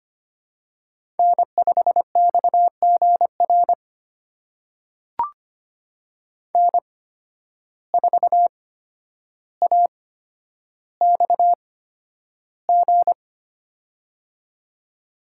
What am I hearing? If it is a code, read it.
Morse code, heard as a keyed tone, about 25 wpm, spaced as for 11 wpm, 700 Hz. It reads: N5XGR N4AXG
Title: USA Call Signs - 5 Characters - 8x Character Spacing - 25wpm